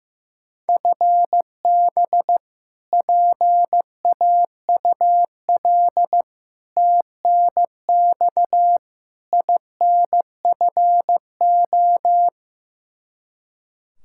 Morse code, heard as a keyed tone, about 15 wpm, 700 Hz. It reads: FB PAUL TNX INFO
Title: Practice QSO - Exchange 3 - More Info - 15wpm